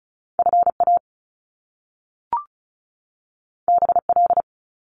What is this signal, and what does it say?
Morse code, heard as a keyed tone, about 36 wpm, 700 Hz. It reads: FA BL
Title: Top Two Letter Combinations in Words 35wpm